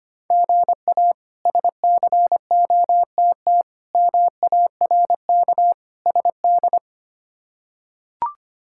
Morse code, heard as a keyed tone, about 25 wpm, 700 Hz. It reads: GA SCOTT MARK SD
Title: K1USN SST QSO Practice - 25wpm